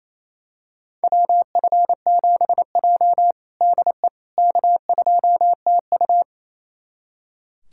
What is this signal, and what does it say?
Morse code, heard as a keyed tone, about 28 wpm, 700 Hz. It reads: WF7J DE K2TU
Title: USA Callsign DE Callsign - 4 Characters - 28wpm